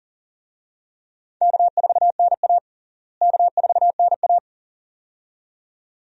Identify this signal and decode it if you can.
Morse code, heard as a keyed tone, about 40 wpm, 700 Hz. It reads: K4NA K4NA
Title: USA Call Signs - 4 Characters - Repeated 2x - 40wpm